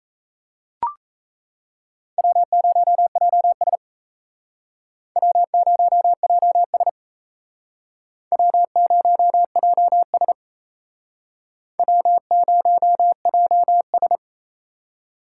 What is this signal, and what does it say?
Morse code, heard as a keyed tone, about 37 wpm, 700 Hz. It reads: W0JS W0JS W0JS W0JS